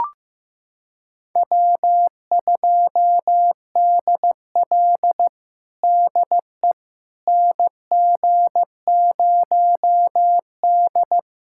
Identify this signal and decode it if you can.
Morse code, heard as a keyed tone, about 15 wpm, 700 Hz. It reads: W2DL DE NG0D
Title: USA Callsign DE Callsign - 4 Characters - 15wpm